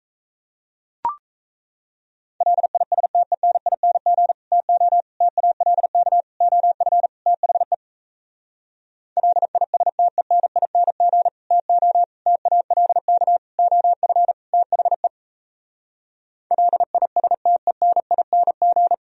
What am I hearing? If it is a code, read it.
Morse code, heard as a keyed tone, about 38 wpm, 700 Hz. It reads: LISTENING TO TALK OF THE LISTENING TO TALK OF THE LISTENING